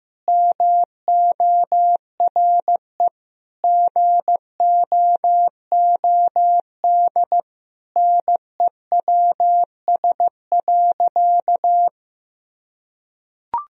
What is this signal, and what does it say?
Morse code, heard as a keyed tone, about 15 wpm, 700 Hz. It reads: MORE GOOD NEWS.
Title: Sentences from Top 1000 Words - Review Part 1 - 15wpm